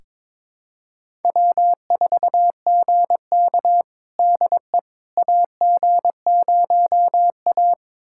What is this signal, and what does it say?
Morse code, heard as a keyed tone, about 21 wpm, 700 Hz. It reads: W4GK DE AG0A
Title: USA Callsign DE Callsign - 4 Characters - 22wpm